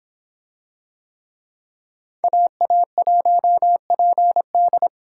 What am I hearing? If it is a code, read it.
Morse code, heard as a keyed tone, about 26 wpm, 700 Hz. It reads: AA1PD